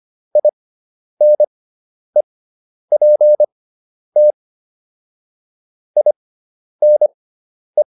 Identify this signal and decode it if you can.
Morse code, heard as a keyed tone, about 25 wpm, spaced as for 10 wpm, 600 Hz. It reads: INEPT INE